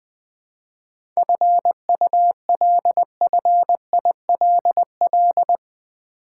Morse code, heard as a keyed tone, about 20 wpm, 700 Hz. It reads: FULFILL
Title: 7 Letter Words - First 3 Letters Given - 20wpm